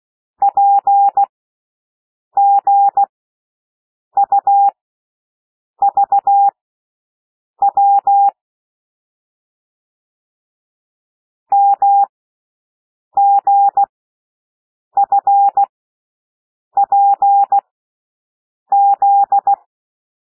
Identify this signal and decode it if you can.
Morse code, heard as a keyed tone, about 16 wpm, spaced as for 6 wpm, 800 Hz. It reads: PGUVW MGFPZ
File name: cwmp3 / L1 SOUND LETTERS 16 wpm